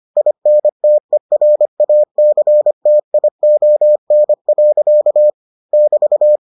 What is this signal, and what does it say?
Morse code, heard as a keyed tone, about 25 wpm, 600 Hz. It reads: INTERACTION. <BT>